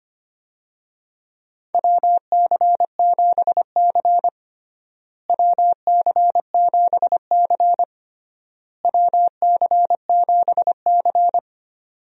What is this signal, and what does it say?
Morse code, heard as a keyed tone, about 25 wpm, 700 Hz. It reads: WC7C WC7C WC7C
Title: USA Call Signs - 4 Characters - Repeated 3x - 25wpm